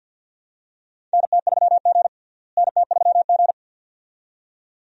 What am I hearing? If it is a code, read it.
Morse code, heard as a keyed tone, about 48 wpm, 700 Hz. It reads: NT3G NT3G
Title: USA Call Signs - 4 Characters - Repeated 2x - 50wpm